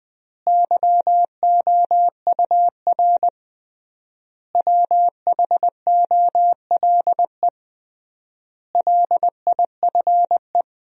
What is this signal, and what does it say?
Morse code, heard as a keyed tone, about 20 wpm, 700 Hz. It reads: YOUR WHOLE LIFE
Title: 3x Word-Spacing - Sentences from Top 400 Words - 20wpm